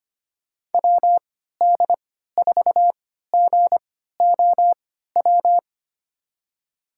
Morse code, heard as a keyed tone, about 25 wpm, spaced as for 14 wpm, 700 Hz. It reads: WD4GOW
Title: USA Call Signs - 3x Character Spacing - 25wpm